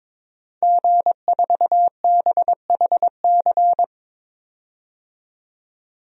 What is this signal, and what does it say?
Morse code, heard as a keyed tone, about 22 wpm, 700 Hz. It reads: G4BHC
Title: DXCC-100 Call Signs - 5 Characters - 22wpm